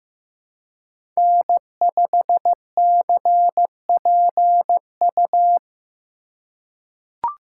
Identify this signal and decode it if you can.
Morse code, heard as a keyed tone, about 15 wpm, 700 Hz. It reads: N5CPU
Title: USA Call Signs - 5 Characters - 4x Character Spacing - 15wpm